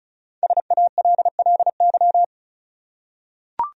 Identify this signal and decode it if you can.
Morse code, heard as a keyed tone, about 35 wpm, 700 Hz. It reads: SALLY
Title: USA Common Names - 35wpm